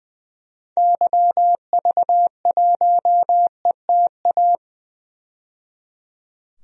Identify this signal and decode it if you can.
Morse code, heard as a keyed tone, about 20 wpm, 700 Hz. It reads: YV1ETA
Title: DXCC-100 Call Signs - 6 Characters - 20wpm